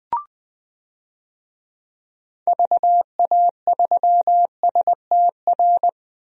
Morse code, heard as a keyed tone, about 20 wpm, 700 Hz. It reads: VA3STR
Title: DXCC-100 Call Signs - 6 Characters - 20wpm